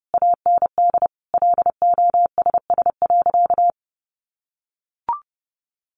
Morse code, heard as a keyed tone, about 30 wpm, 700 Hz. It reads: AND LOSS.
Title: Sentences from Top 1400 Words 30wpm